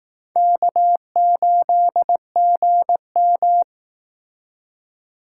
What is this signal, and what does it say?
Morse code, heard as a keyed tone, about 19 wpm, 700 Hz. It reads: K8GM